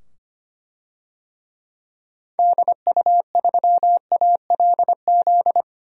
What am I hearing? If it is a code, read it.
Morse code, heard as a keyed tone, about 25 wpm, 700 Hz. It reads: DU3ALZ
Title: DXCC-100 Call Signs - 6 Characters - 25wpm